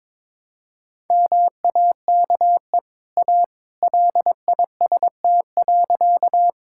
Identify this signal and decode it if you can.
Morse code, heard as a keyed tone, about 22 wpm, 690 Hz. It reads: MAKE A LIST.
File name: practice / 22wpm / Easy Sentences with Hints - 22wpm